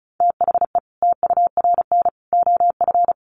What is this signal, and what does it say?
Morse code, heard as a keyed tone, about 34 wpm, 700 Hz. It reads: THE TURN OF
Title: Sets of 5 Words 35wpm